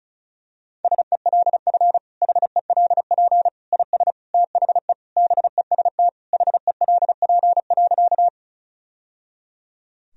Morse code, heard as a keyed tone, about 36 wpm, 700 Hz. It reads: SELF HELP IS THE BEST HELP.
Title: Sentences from Top 1000 Words - Review Part 1 - 35wpm